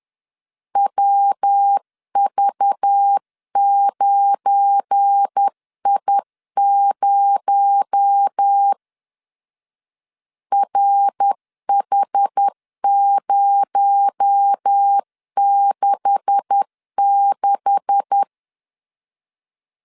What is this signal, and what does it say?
Morse code, heard as a keyed tone, about 10 wpm, 800 Hz. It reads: WV9I0 RH066
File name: LEZIONI / 9 286 10wpm 2735